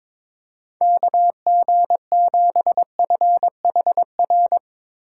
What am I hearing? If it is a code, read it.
Morse code, heard as a keyed tone, about 22 wpm, 700 Hz. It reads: KG7FHR
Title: USA Call Signs - 3x Character Spacing - 22wpm